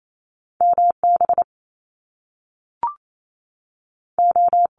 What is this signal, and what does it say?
Morse code, heard as a keyed tone, about 28 wpm, 700 Hz. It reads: MB O
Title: Top Two Letter Combinations in Words 28wpm